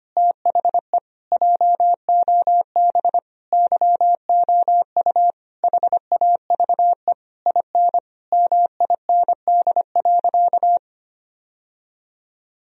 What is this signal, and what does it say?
Morse code, heard as a keyed tone, about 25 wpm, 700 Hz. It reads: THE JOB YOU HAVE IN MIND.
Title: Sentences from Top 1000 Words - Review Part 3 - 25wpm